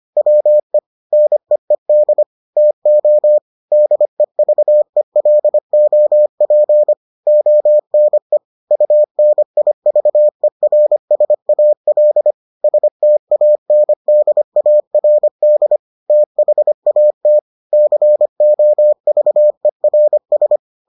Morse code, heard as a keyed tone, about 25 wpm, 600 Hz. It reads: WE NEED TO DEVELOP ONE UNIVERSAL STANDARD THAT COVERS